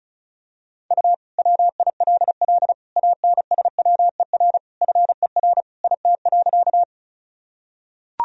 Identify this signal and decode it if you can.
Morse code, heard as a keyed tone, about 34 wpm, 700 Hz. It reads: U WILL ANSWER FER IT.
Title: Sentences from Top 300 Words and Abbreviations 35wpm